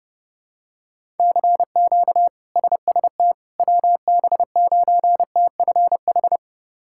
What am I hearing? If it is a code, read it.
Morse code, heard as a keyed tone, about 32 wpm, 700 Hz. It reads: CQ SST WB9TFH
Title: K1USN SST QSO Practice - 30wpm